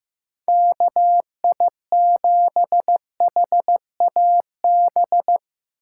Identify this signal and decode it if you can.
Morse code, heard as a keyed tone, about 17 wpm, 700 Hz. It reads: KI7HAB
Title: Speed Racing - USA 2x3 Call Signs 15wpm